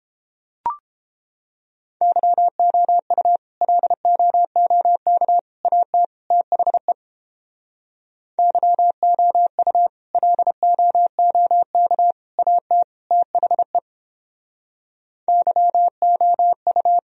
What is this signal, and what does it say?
Morse code, heard as a keyed tone, about 29 wpm, 700 Hz. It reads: YOU LOOK AT THE YOU LOOK AT THE YOU